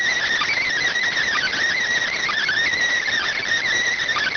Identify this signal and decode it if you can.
Emergency Alert System (SAME) tones, heard as harsh buzzing bursts. It is heard from 0.5 to 4.0 s.